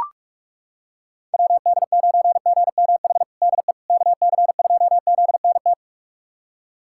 Morse code, heard as a keyed tone, about 41 wpm, 700 Hz. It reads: WD9GMH DE KC2ZNT
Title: USA Callsign DE Callsign 45wpm